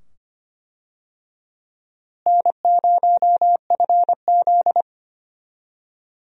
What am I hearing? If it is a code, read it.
Morse code, heard as a keyed tone, about 25 wpm, 700 Hz. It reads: N0FZ
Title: USA Call Signs - 4 Characters - Repeated 2x - 25wpm